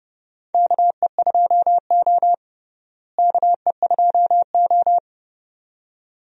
Morse code, heard as a keyed tone, about 30 wpm, 700 Hz. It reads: KE2O KE2O
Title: USA Call Signs - 4 Characters - Repeated 2x - 30wpm